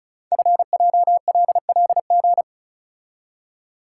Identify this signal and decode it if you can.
Morse code, heard as a keyed tone, about 36 wpm, 700 Hz. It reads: FJLLG
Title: Code Groups - 5 Letters - 35wpm